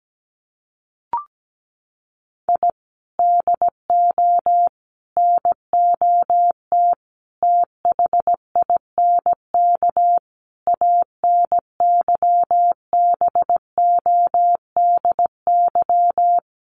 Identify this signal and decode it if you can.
Morse code, heard as a keyed tone, about 17 wpm, 700 Hz. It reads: I DO NOT THINK ANYBODY
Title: Sets of 5 Words 17wpm